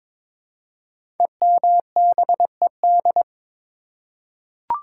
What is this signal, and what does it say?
Morse code, heard as a keyed tone, about 22 wpm, 700 Hz. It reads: EMBED
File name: 5 Letter Words - First 3 Letters Given - 22wpm